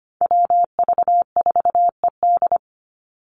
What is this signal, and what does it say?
Morse code, heard as a keyed tone, about 25 wpm, 700 Hz. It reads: WV4ED
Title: USA Call Signs 25wpm